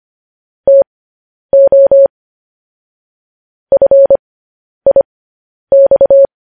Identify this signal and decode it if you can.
Morse code, heard as a keyed tone, about 25 wpm, spaced as for 10 wpm, 550 Hz. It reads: TO FIX